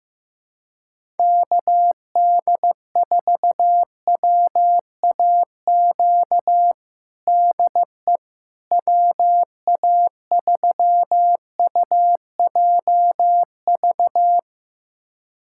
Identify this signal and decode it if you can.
Morse code, heard as a keyed tone, about 15 wpm, 700 Hz. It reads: KD4WAQ DE WA3UJV
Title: USA Callsign DE Callsign 15wpm